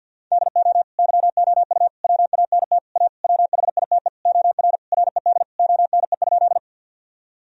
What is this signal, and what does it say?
Morse code, heard as a keyed tone, about 50 wpm, 700 Hz. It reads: DO YOU WANT A WHITE OR RED ONE?
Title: Sentences from Top 1000 Words - Review Part 2 - 50wpm